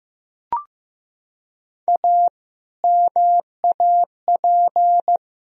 Morse code, heard as a keyed tone, about 15 wpm, 700 Hz. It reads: A MAP
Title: Sentences from Top 1000 Words - Review Part 2 - 15wpm